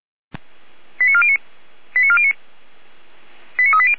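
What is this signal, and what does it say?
Five-tone selective call (ZVEI), heard as quick runs of five tones, tones of about 70 ms. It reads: 89409 89409 89380
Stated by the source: ZVEI five-tone selective call 89340, received by SDR.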